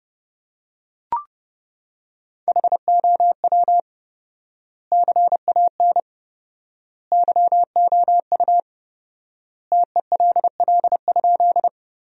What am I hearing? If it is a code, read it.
Morse code, heard as a keyed tone, about 30 wpm, 700 Hz. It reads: HOW CAN YOU TELL?
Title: 4x Word-Spacing - Sentences from Top 200 Words - 30wpm